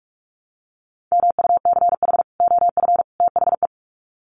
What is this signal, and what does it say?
Morse code, heard as a keyed tone, about 44 wpm, 700 Hz. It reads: MUCH OF THE
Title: Sets of 3 Words 45wpm